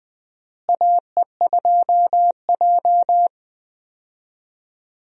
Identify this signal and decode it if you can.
Morse code, heard as a keyed tone, about 20 wpm, 700 Hz. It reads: AE2J
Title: USA Call Signs - 4 Characters - Repeated 3x - 20wpm